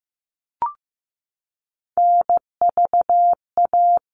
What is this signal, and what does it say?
Morse code, heard as a keyed tone, about 15 wpm, 700 Hz. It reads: NVA
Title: All Cut Numbers - 3-Digit 15wpm